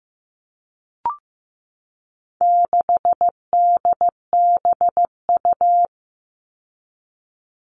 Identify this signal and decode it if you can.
Morse code, heard as a keyed tone, about 15 wpm, 700 Hz. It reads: 6DBU
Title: All Cut Numbers - 4-Digit 15wpm